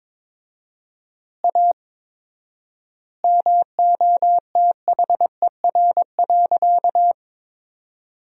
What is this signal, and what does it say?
Morse code, heard as a keyed tone, about 22 wpm, 700 Hz. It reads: A MOTHER.
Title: 4x Word-Spacing - Sentences from Top 200 Words - 22wpm